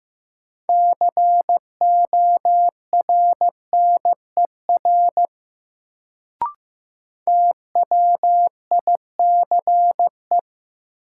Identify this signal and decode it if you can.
Morse code, heard as a keyed tone, about 15 wpm, 700 Hz. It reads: CORNER TWICE
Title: Sentences from Top 1600 Words - New Word Drill - 15wpm